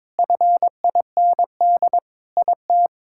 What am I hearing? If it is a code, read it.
Morse code, heard as a keyed tone, about 22 wpm, 700 Hz. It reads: FIND IT
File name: practice / 22wpm / Sentences with 3-Word Phrases - 22wpm